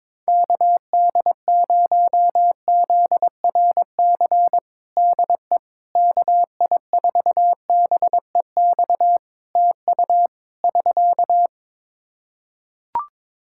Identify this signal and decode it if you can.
Morse code, heard as a keyed tone, about 22 wpm, 700 Hz. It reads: KD0ZRC DE KI4BEX TU <SK>
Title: Practice QSO - Exchange 4 - Ending - 22wpm